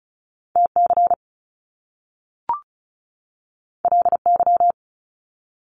Morse code, heard as a keyed tone, about 36 wpm, 700 Hz. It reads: TC LY
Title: Top Two Letter Combinations in Words 35wpm